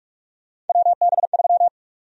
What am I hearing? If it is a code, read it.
Morse code, heard as a keyed tone, about 42 wpm, 700 Hz. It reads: WB3
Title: USA Call Sign Prefixes - Repeated 2x - 45wpm